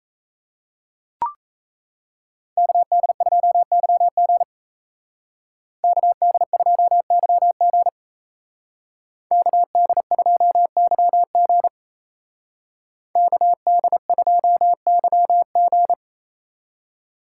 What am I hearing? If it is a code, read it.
Morse code, heard as a keyed tone, about 35 wpm, 700 Hz. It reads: KD2YG KD2YG KD2YG KD2YG
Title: Speed Racing - USA 2x2 Call Signs 28wpm